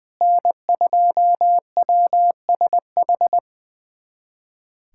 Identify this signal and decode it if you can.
Morse code, heard as a keyed tone, about 21 wpm, 700 Hz. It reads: N2WSH
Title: USA Callsign DE Callsign - 5 Characters - 20wpm